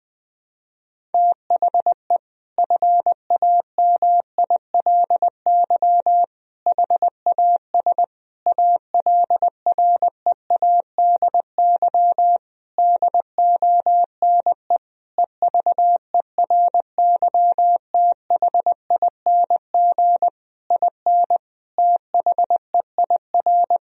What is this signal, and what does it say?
Morse code, heard as a keyed tone, about 20 wpm, 700 Hz. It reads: THE FAMILY HAS ALREADY DONE EVERYTHING IN THEIR
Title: Sentences from Top 1000 Words - Review Part 2 - 20wpm